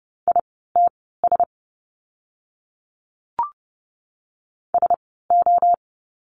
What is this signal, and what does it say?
Morse code, heard as a keyed tone, about 30 wpm, spaced as for 17 wpm, 700 Hz. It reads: ITS SO
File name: course / 30:10wpm / 230 Lesson 37f - Letter x - Words 30wpm